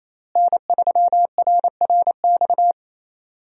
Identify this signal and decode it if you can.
Morse code, heard as a keyed tone, about 28 wpm, 700 Hz. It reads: N3RRX
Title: USA Call Signs - 5 Characters - Repeated 2x - 28wpm